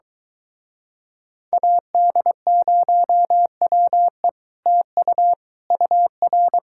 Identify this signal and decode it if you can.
Morse code, heard as a keyed tone, about 23 wpm, 700 Hz. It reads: AD0WE TU UR